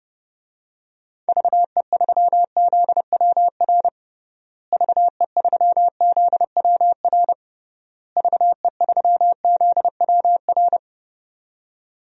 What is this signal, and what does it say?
Morse code, heard as a keyed tone, about 30 wpm, 700 Hz. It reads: VE3ZWR VE3ZWR VE3ZWR
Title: DXCC-100 Call Signs - 6 Characters - Repeated 3x - 30wpm